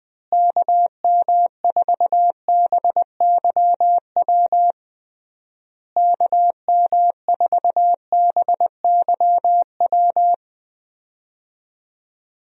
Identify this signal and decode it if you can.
Morse code, heard as a keyed tone, about 20 wpm, 700 Hz. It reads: KM4BYW KM4BYW
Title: USA Call Signs - 6 Characters - Repeated 3x - 20wpm